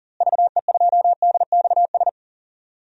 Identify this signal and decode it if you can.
Morse code, heard as a keyed tone, about 39 wpm, 700 Hz. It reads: VE2DXS
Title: DXCC-100 Call Signs - 6 Characters - Repeated 3x - 40wpm